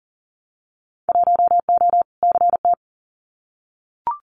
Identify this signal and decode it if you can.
Morse code, heard as a keyed tone, about 40 wpm, 700 Hz. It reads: 1O CT
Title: Winter Field Day Exchanges 40wpm